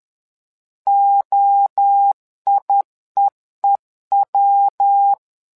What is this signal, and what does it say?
Morse code, heard as a keyed tone, about 11 wpm, 800 Hz. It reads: OIEEW
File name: LEZIONI / 3 MWRO / 3 MWRO 10wpm 2148